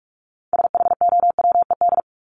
Morse code, heard as a keyed tone, about 44 wpm, 700 Hz. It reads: SHOWED
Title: ED Suffix - Top 100 Words with Base Words 45wpm